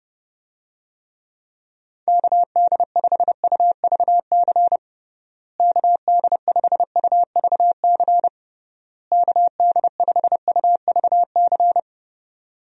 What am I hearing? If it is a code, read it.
Morse code, heard as a keyed tone, about 30 wpm, 700 Hz. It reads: KD5UVC KD5UVC KD5UVC
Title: USA Call Signs - 6 Characters - Repeated 3x - 30wpm